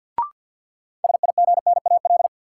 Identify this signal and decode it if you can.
Morse code, heard as a keyed tone, about 53 wpm, 700 Hz. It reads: SIGNAL
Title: Sentences from Top 1400 Words - New Word Drill - 50wpm